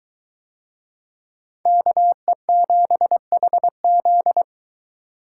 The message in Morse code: KE7HZ

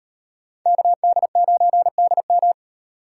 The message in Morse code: KD9DM

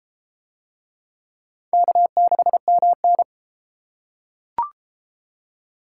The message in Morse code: K6MN

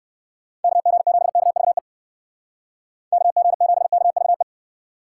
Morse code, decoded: KG7KVE KG7KVE

